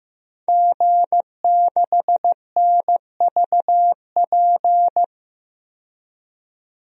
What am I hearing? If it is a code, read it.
Morse code, heard as a keyed tone, about 16 wpm, 700 Hz. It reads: G6NVP
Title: DXCC-100 Call Signs - 5 Characters - 15wpm